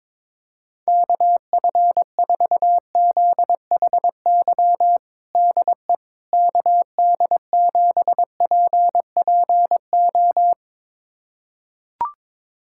Morse code, heard as a keyed tone, about 22 wpm, 700 Hz. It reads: KF4ZHY DE KD7PPO
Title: USA Callsign DE Callsign 22wpm